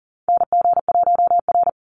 Morse code, heard as a keyed tone, about 40 wpm, 700 Hz. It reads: NG1R
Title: Call Signs Repeated 2x - 40wpm